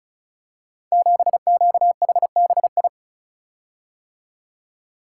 Morse code, heard as a keyed tone, about 36 wpm, 700 Hz. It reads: 7QHBI
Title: Code Groups - 5 Letter:Numbers - 35wpm